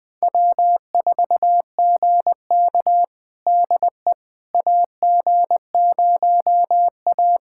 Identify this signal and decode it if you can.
Morse code, heard as a keyed tone, about 20 wpm, 700 Hz. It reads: W4GK DE AG0A
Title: USA Callsign DE Callsign - 4 Characters - 20wpm